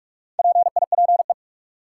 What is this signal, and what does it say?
Morse code, heard as a keyed tone, about 42 wpm, 700 Hz. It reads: PIPE
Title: Sentences from Top 1500 Words - New Word Drill Alt - 45wpm